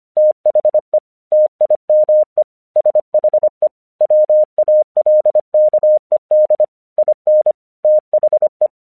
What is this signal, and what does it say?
Morse code, heard as a keyed tone, about 25 wpm, 610 Hz. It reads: THE TIME SHE WALKED IN THE
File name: 1 English 25wpm / AQC23